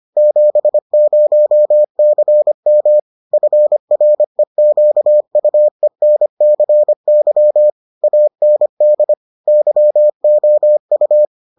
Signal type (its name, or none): Morse code